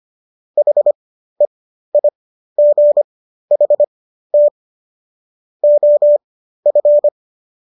Morse code, heard as a keyed tone, about 25 wpm, spaced as for 13 wpm, 590 Hz. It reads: HEIGHT OF